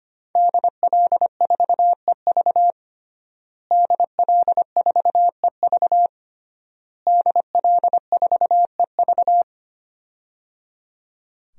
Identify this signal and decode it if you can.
Morse code, heard as a keyed tone, about 25 wpm, 700 Hz. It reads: DL4EV DL4EV DL4EV
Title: DXCC-100 Call Signs - 5 Characters - Repeated 3x - 25wpm